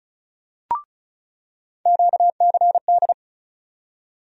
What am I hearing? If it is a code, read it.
Morse code, heard as a keyed tone, about 36 wpm, 700 Hz. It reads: QCD
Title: USA Call Sign Postfixes - 3 Letters 35wpm